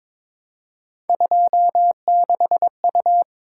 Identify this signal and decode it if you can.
Morse code, heard as a keyed tone, about 23 wpm, 700 Hz. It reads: 26U
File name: practice / 22wpm / Three Letter-Number Combinations - 22wpm